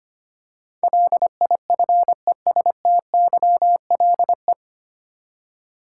Morse code, heard as a keyed tone, about 25 wpm, 700 Hz. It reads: LIFESTYLE
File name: Nine Letter Words - 25wpm